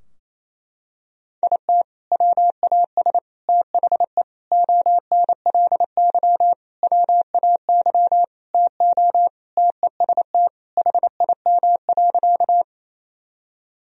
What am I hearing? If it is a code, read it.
Morse code, heard as a keyed tone, about 28 wpm, 700 Hz. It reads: IT WAS THE ONLY WAY TO TEST HIM.
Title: Sentences from Top 1000 Words - Review Part 3 - 28wpm